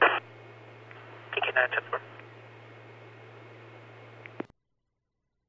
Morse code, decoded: none

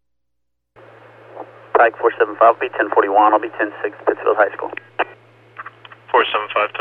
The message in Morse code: none

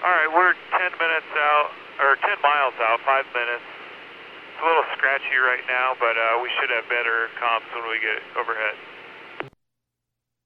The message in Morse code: none

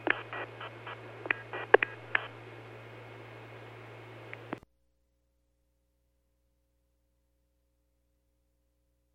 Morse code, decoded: none

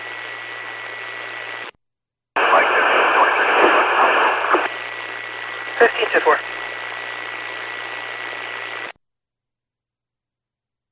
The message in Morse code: none

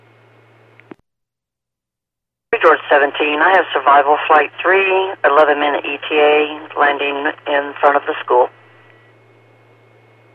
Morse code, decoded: none

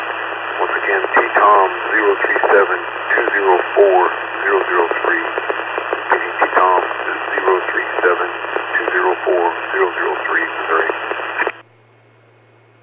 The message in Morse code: none

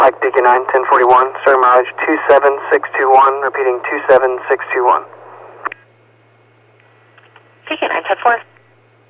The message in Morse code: none

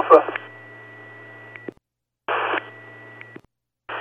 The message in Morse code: none

